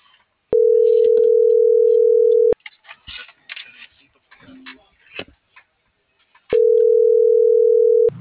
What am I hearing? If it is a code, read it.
Ringback tone (North America), 440 + 480 Hz.